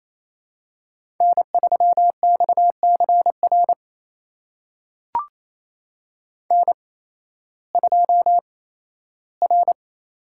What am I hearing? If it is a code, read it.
Morse code, heard as a keyed tone, about 28 wpm, spaced as for 15 wpm, 700 Hz. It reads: N3XCR N2R
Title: USA Call Signs - 5 Characters - 8x Character Spacing - 28wpm